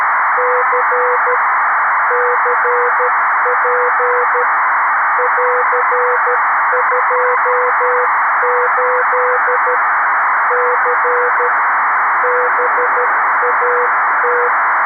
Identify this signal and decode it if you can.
Morse code, heard as a keyed tone, about 13 wpm, 500 Hz, with noise. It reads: C CP <AR>28 C BAT
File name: arissat / recordings / ARISSat 11Sept2011 1354utc 60frames